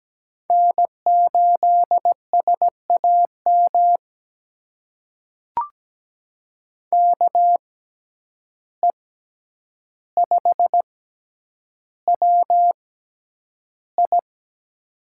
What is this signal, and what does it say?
Morse code, heard as a keyed tone, about 17 wpm, spaced as for 9 wpm, 700 Hz. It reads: N8SAM KE5WI
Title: USA Call Signs - 5 Characters - 6x Character Spacing - 17wpm